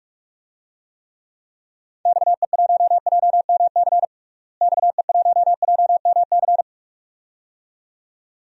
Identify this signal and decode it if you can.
Morse code, heard as a keyed tone, about 46 wpm, 690 Hz. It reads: XE1JMC XE1JMC
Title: DXCC-100 Call Signs - 6 Characters - Repeated 2x - 45wpm